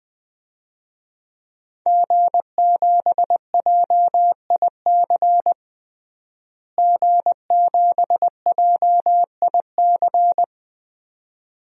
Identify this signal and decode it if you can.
Morse code, heard as a keyed tone, about 21 wpm, 700 Hz. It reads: G7JIC G7JIC
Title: DXCC-100 Call Signs - 5 Characters - Repeated 2x - 20wpm